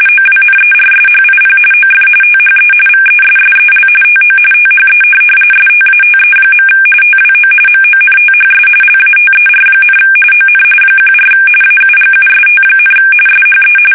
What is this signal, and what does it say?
RTTY, about 45.45 baud. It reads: OWNERSHIP OF THE VESSEL TO THE STATE. TWO MONTHS LATER, ON 14 AUGUST, SHE
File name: fsk850 / USS-Massachusetts